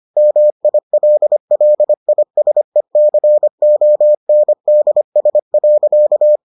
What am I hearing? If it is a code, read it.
Morse code, heard as a keyed tone, about 25 wpm, 600 Hz. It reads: MILLISECONDS.